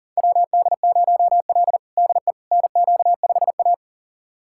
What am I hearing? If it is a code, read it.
Morse code, heard as a keyed tone, about 40 wpm, 700 Hz. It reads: WD0L DE NQ5A
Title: USA Callsign DE Callsign - 4 Characters - 40wpm